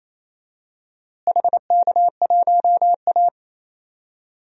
Morse code, heard as a keyed tone, about 28 wpm, 700 Hz. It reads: HK1A
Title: DXCC-100 Call Signs - 4 Characters - Repeated 2x - 28wpm